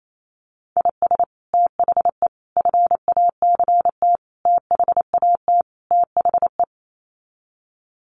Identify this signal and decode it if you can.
Morse code, heard as a keyed tone, about 28 wpm, 700 Hz. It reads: IS THE FACT THAT THE
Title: Sets of 5 Words Encore2 28wpm